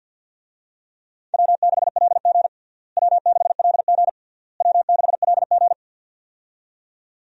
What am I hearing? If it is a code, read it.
Morse code, heard as a keyed tone, about 52 wpm, 700 Hz. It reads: W6LG W6LG W6LG
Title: USA Call Signs - 4 Characters - Repeated 3x - 50wpm